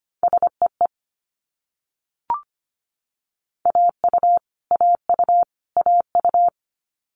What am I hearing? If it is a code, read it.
Morse code, heard as a keyed tone, about 25 wpm, 700 Hz. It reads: SEE AU AU AU
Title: Multi-letter Phonemes Repeated 3x 25wpm